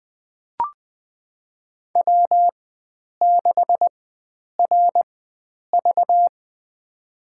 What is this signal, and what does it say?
Morse code, heard as a keyed tone, about 20 wpm, spaced as for 9 wpm, 700 Hz. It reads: W6RV
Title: USA Call Signs - 4 Characters - 4x Character Spacing - 20wpm